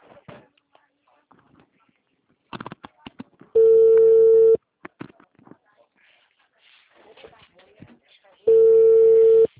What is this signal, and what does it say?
Ringback tone (China, 450 Hz), with speech.
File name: out- -2009- - - .80449